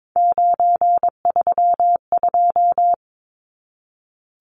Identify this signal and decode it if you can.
Morse code, heard as a keyed tone, about 22 wpm, 700 Hz. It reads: 932
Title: Numbers - 3-Digit 22wpm